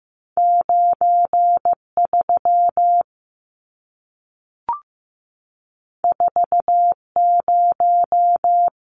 Morse code, heard as a keyed tone, about 15 wpm, 700 Hz. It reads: 93 40
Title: Numbers - 2-Digit 15wpm